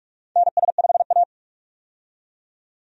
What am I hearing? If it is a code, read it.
Morse code, heard as a keyed tone, about 48 wpm, 700 Hz. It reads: NS5A
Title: USA Call Signs - 4 Characters - 45wpm